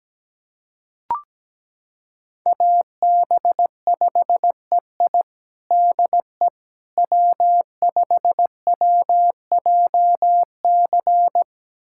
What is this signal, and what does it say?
Morse code, heard as a keyed tone, about 17 wpm, 700 Hz. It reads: AB5EI DE W5WJC
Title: USA Callsign DE Callsign - 5 Characters - 17wpm